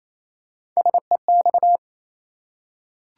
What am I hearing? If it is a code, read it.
Morse code, heard as a keyed tone, about 28 wpm, 700 Hz. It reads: SEX
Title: Three Letter Words - 28wpm